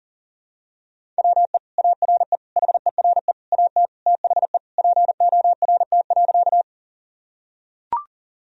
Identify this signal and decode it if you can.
Morse code, heard as a keyed tone, about 40 wpm, 700 Hz. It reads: WE ARE HERE AT THE PORT.
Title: Sentences from Top 1000 Words - Review Part 1 - 40wpm